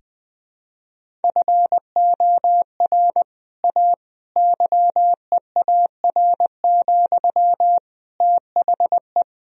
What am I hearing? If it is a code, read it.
Morse code, heard as a keyed tone, about 20 wpm, 700 Hz. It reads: FOR A YEAR, THE
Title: Sentences from Top 1000 Words - Review Part 1 - 20wpm